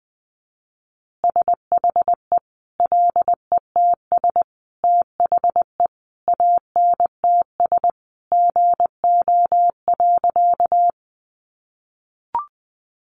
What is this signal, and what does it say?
Morse code, heard as a keyed tone, about 20 wpm, 700 Hz. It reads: SHE LETS THE ANTS GO.
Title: Easy Phrases 20wpm